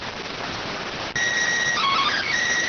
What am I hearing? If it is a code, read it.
SSTV (slow-scan television), PD 290.